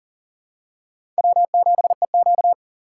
Morse code, heard as a keyed tone, about 37 wpm, 690 Hz. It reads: W7EQ
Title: USA Call Signs - 4x Character Spacing - 40wpm